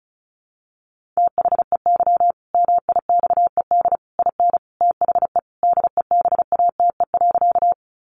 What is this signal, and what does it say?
Morse code, heard as a keyed tone, about 34 wpm, 700 Hz. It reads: THEY MIXED IN THE DEBATE.
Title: Sentences from Top 1500 Words 35wpm